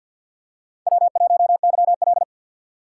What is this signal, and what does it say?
Morse code, heard as a keyed tone, about 47 wpm, 700 Hz. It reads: W1YL
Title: USA Call Signs - 4 Characters - 2x Character Spacing - 50wpm